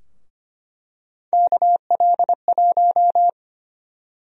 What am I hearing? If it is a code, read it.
Morse code, heard as a keyed tone, about 25 wpm, 700 Hz. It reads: KL1
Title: USA Call Sign Prefixes - Repeated 2x - 25wpm